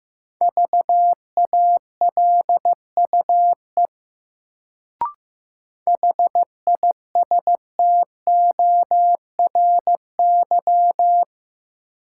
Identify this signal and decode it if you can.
Morse code, heard as a keyed tone, about 15 wpm, 700 Hz. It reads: VALUE HISTORY
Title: Sentences from Top 600 Words - New Word Drill - 15wpm